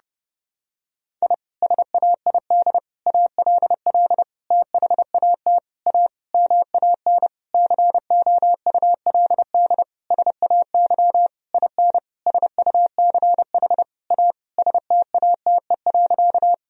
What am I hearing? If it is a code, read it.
Morse code, heard as a keyed tone, about 30 wpm, 700 Hz. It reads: I SAID ALL THAT A MAN COULD SAY IN SUCH A STATE.
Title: Sentences from Top 1000 Words - Review Part 2 - 30wpm